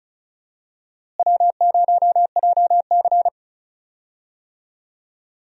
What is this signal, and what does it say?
Morse code, heard as a keyed tone, about 36 wpm, 700 Hz. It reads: W0JC
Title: USA Call Signs - 4 Characters - Repeated 3x - 35wpm